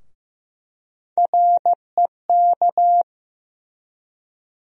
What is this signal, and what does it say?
Morse code, heard as a keyed tone, about 15 wpm, 700 Hz. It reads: REK
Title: USA Call Sign Postfixes - 3 Letters 15wpm